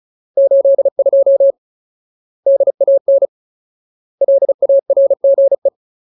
Morse code, heard as a keyed tone, about 34 wpm, 550 Hz. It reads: 82 DAN LARGE
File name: Ultimate Word List / 35WPM Ultimate